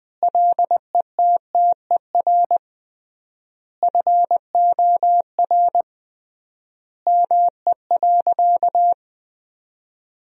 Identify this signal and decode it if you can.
Morse code, heard as a keyed tone, about 20 wpm, 700 Hz. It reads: LETTER FOR ME.
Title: 3x Word-Spacing - Sentences from Top 300 Words - 20wpm